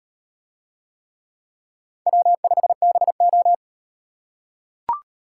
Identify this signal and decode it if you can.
Morse code, heard as a keyed tone, about 38 wpm, 700 Hz. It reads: W5BO